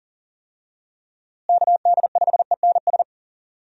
Morse code, heard as a keyed tone, about 40 wpm, 700 Hz. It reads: KD5ENS
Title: USA Call Signs - 6 Characters - Repeated 2x - 40wpm